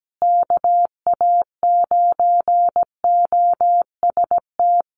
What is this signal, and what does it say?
Morse code, heard as a keyed tone, about 17 wpm, 700 Hz. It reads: KA9OST
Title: USA Call Signs Repeated 2x 17wpm